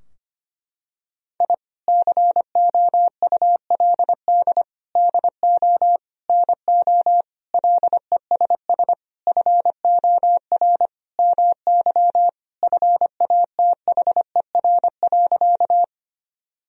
Morse code, heard as a keyed tone, about 25 wpm, 700 Hz. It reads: I COULD DO NO LESS FOR MY FATHER.
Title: Sentences from Top 1000 Words - Review Part 2 - 25wpm